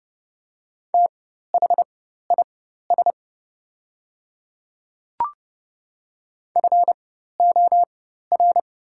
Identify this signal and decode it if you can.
Morse code, heard as a keyed tone, about 30 wpm, spaced as for 14 wpm, 700 Hz. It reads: THIS FOR
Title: 4x Character Spacing - Top 200 Words - 30wpm